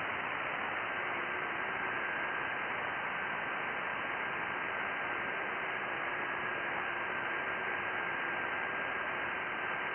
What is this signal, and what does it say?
FT8.
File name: IC-7100